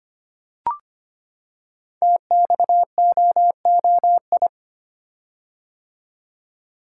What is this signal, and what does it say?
Morse code, heard as a keyed tone, about 25 wpm, 700 Hz. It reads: TXOOI
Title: Code Groups - 5 Letters - 25wpm